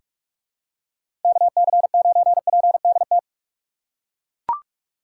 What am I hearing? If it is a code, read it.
Morse code, heard as a keyed tone, about 42 wpm, 700 Hz. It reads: KC9PDT
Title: USA Call Signs - 6 Characters - 4x Character Spacing - 45wpm